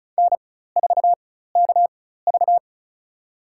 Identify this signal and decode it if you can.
Morse code, heard as a keyed tone, about 36 wpm, spaced as for 16 wpm, 700 Hz. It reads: N4KV
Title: USA Call Signs - 4x Character Spacing - 35wpm